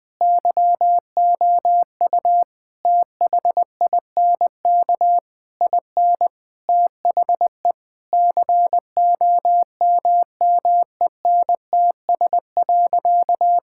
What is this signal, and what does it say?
Morse code, heard as a keyed tone, about 21 wpm, 700 Hz. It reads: YOU THINK IN THE COMMENTS.
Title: Sentences with 8-Word Phrases - 20wpm